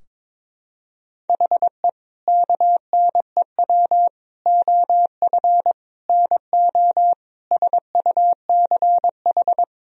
Morse code, heard as a keyed tone, about 22 wpm, 700 Hz. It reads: HE KNEW OF NO SUCH
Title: Sentences from Top 1000 Words - Review Part 2 - 22wpm